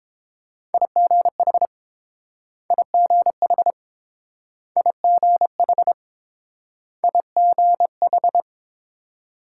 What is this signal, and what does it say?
Morse code, heard as a keyed tone, about 30 wpm, 700 Hz. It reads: IGH IGH IGH IGH